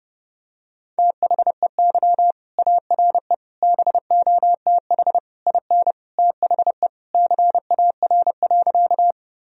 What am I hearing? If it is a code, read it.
Morse code, heard as a keyed tone, about 30 wpm, 700 Hz. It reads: THEY ARE BOTH IN THE CAR.
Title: Sentences from Top 1000 Words - Review Part 1 - 30wpm